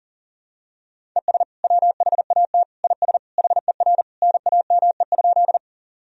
Morse code, heard as a keyed tone, about 40 wpm, 700 Hz. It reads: ES WHAT IS HER NAME?
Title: Sentences from Top 200 Words and Abbreviations 40wpm